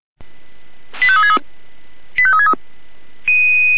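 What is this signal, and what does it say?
Five-tone selective call (ZVEI), heard as quick runs of five tones, tones of about 70 ms. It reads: 85363 85363
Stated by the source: ZVEI five-tone selective call received on an SDR.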